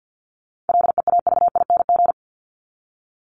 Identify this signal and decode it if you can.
Morse code, heard as a keyed tone, about 50 wpm, 700 Hz. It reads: LEAVING
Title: ING Suffix - Top 100 Words with Base Words 50wpm